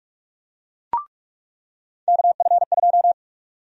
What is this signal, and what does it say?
Morse code, heard as a keyed tone, about 46 wpm, 700 Hz. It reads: KF2
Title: USA Call Sign Prefixes - 45wpm